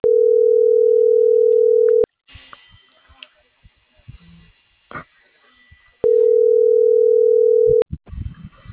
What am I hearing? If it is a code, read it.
Ringback tone (North America), 440 + 480 Hz.
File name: CENTRAL NMP / out brayan.gutierreznmp